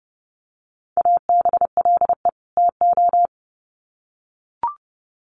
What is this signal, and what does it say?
Morse code, heard as a keyed tone, about 30 wpm, 700 Hz. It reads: ABLE TO
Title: Sets of 2 Words 30wpm